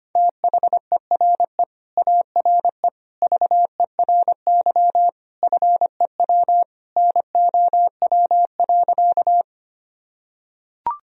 Morse code, heard as a keyed tone, about 25 wpm, 700 Hz. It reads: THERE ARE VERY FEW NOW.
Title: Sentences from Top 1000 Words - Review Part 1 - 25wpm